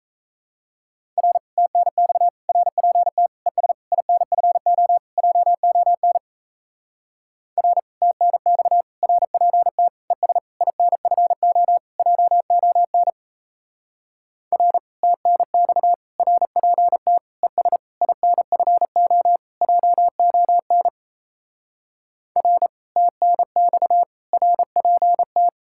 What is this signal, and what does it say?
Morse code, heard as a keyed tone, about 34 wpm, 700 Hz. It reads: R TNX RPT ES INFO JON R TNX RPT ES INFO JON R TNX RPT ES INFO JON R TNX RPT
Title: Speed-Racing Common QSO Phrases - 28wpm